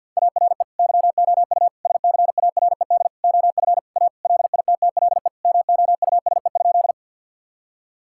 Morse code, heard as a keyed tone, about 50 wpm, 700 Hz. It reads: ARE YOU SCARED OF A LITTLE MOUSE?
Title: Sentences from Top 1000 Words - Review Part 5 - 50wpm